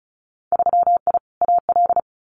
Morse code, heard as a keyed tone, about 36 wpm, 700 Hz. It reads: 3I AL
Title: Winter Field Day Exchanges Encore 35wpm